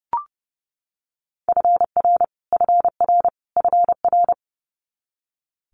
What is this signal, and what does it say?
Morse code, heard as a keyed tone, about 30 wpm, 700 Hz. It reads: FR FR FR
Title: Multi-letter Phonemes Repeated 3x 30wpm